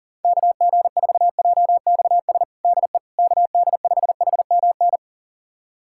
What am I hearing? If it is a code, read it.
Morse code, heard as a keyed tone, about 40 wpm, 700 Hz. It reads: KG4JXS DE KD5HMN
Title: USA Callsign DE Callsign - 6 Characters - 40wpm